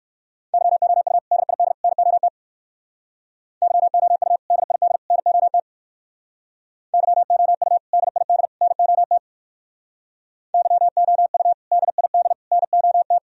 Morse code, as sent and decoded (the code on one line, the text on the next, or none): -.-- --- ..- / -.. .. -.. / -. --- - / -.-- --- ..- / -.. .. -.. / -. --- - / -.-- --- ..- / -.. .. -.. / -. --- - / -.-- --- ..- / -.. .. -.. / -. --- -
YOU DID NOT YOU DID NOT YOU DID NOT YOU DID NOT